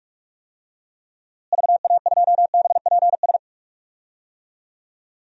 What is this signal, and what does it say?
Morse code, heard as a keyed tone, about 46 wpm, 700 Hz. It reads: VA2BPS
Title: DXCC-100 Call Signs - 6 Characters - Repeated 2x - 45wpm